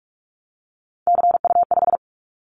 Morse code, heard as a keyed tone, about 46 wpm, 700 Hz. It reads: CU5
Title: Callsign Prefixes for Top 50 DXCC - 45wpm